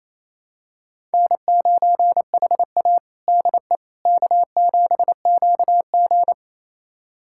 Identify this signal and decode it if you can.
Morse code, heard as a keyed tone, about 28 wpm, 700 Hz. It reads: N9HA DE K7QG
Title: USA Callsign DE Callsign - 4 Characters - 28wpm